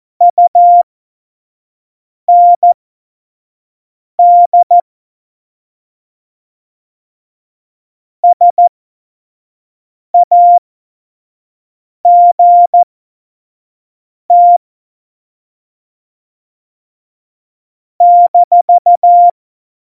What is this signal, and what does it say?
Morse code, heard as a keyed tone, about 14 wpm, spaced as for 5 wpm, 700 Hz. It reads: UND SAGT -
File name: Morsekurs / Witze / Witz 25 Juli 17, Tempo 25 F70